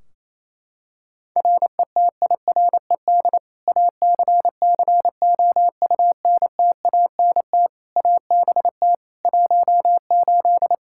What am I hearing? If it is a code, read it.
Morse code, heard as a keyed tone, about 28 wpm, 700 Hz. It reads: RETIRED ACCOUNTANT ABT 18
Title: K1USN SST QSO Practice 28wpm